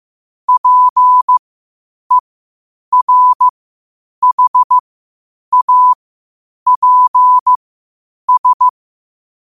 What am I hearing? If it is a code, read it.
Morse code, heard as a keyed tone, about 15 wpm, spaced as for 8 wpm, 1000 Hz. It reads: PERHAPS
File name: qotdfiles / 8wpm